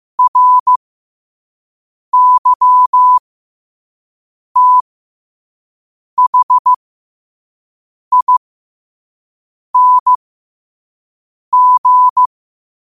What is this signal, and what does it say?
Morse code, heard as a keyed tone, about 15 wpm, spaced as for 5 wpm, 1000 Hz. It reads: RYTHING
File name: qotdfiles / 5wpm